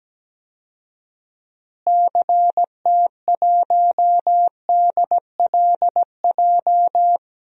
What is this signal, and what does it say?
Morse code, heard as a keyed tone, about 17 wpm, 700 Hz. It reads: CT1DLJ
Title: DXCC-100 Call Signs - 6 Characters - Repeated 3x - 17wpm